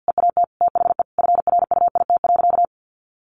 Morse code, heard as a keyed tone, about 50 wpm, 700 Hz. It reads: EAT THE FRUIT.
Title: Sentences from Top 1300 Words 50wpm